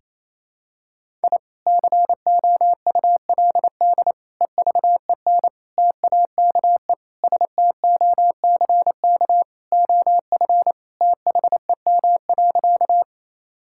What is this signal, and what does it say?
Morse code, heard as a keyed tone, about 28 wpm, 700 Hz. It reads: I COULD EVEN TAKE STOCK OF THEM.
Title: Sentences from Top 1000 Words - Review Part 5 - 28wpm